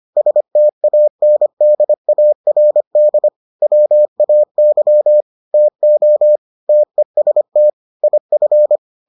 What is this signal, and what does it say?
Morse code, heard as a keyed tone, about 25 wpm, 600 Hz. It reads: STANDARD WAY TO TEST IF